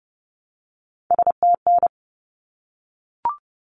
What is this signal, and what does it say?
Morse code, heard as a keyed tone, about 30 wpm, 700 Hz. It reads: STN
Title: Abbreviations 30wpm